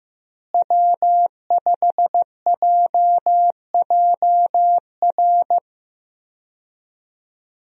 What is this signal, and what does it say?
Morse code, heard as a keyed tone, about 15 wpm, 700 Hz. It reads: W5JJR
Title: USA Call Signs - 5 Characters - Repeated 3x - 15wpm